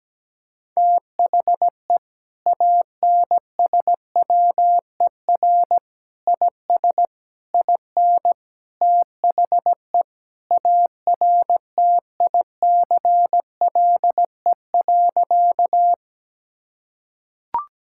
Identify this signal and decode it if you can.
Morse code, heard as a keyed tone, about 17 wpm, 700 Hz. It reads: THE ANSWER IS IN THE ARTICLE.
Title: Sentences from Top 1000 Words - Review Part 5 - 17wpm